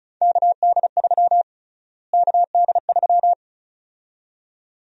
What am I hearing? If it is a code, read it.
Morse code, heard as a keyed tone, about 34 wpm, 700 Hz. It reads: KD3 KD3
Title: USA Call Sign Prefixes - Repeated 2x - 35wpm